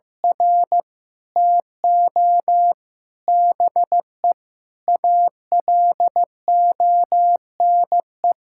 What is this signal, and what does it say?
Morse code, heard as a keyed tone, about 15 wpm, 700 Hz. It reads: R TO BE ALONE
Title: Sentences from Top 1000 Words - Review Part 5 - 15wpm